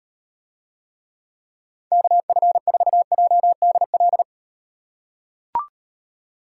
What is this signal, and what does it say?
Morse code, heard as a keyed tone, about 38 wpm, 700 Hz. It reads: KF4JDL